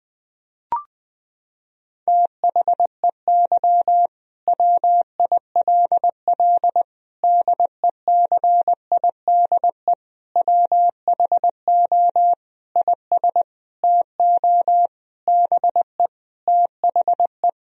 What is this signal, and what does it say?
Morse code, heard as a keyed tone, about 21 wpm, 700 Hz. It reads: THEY WILL DECIDE WHO IS TO BE THE
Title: Sentences from Top 1000 Words - Review Part 3 - 20wpm